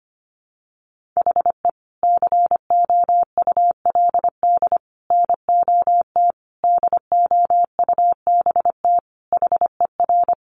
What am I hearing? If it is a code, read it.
Morse code, heard as a keyed tone, about 25 wpm, 700 Hz. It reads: HE COULD NOT DOUBT HER
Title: Sentences from Top 1500 Words 25wpm